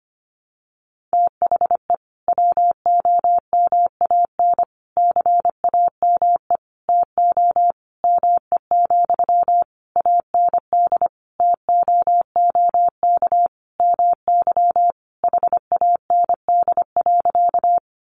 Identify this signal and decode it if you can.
Morse code, heard as a keyed tone, about 25 wpm, 690 Hz. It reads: THE WOMAN CAME TO ME, AND TOOK MY HAND.
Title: Sentences from Top 1100 Words 25wpm